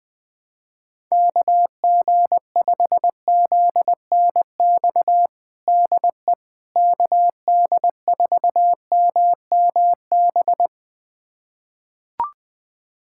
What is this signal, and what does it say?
Morse code, heard as a keyed tone, about 21 wpm, 700 Hz. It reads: KG5ZNX DE KD4MMB
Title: USA Callsign DE Callsign 20wpm